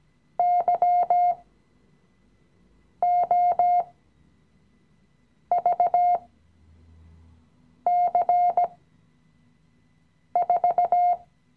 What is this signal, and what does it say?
Morse code, heard as a keyed tone, about 17 wpm, spaced as for 5 wpm, 710 Hz. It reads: YOVC4
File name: CW-Online-Kurs / Anfänger / Lektion 017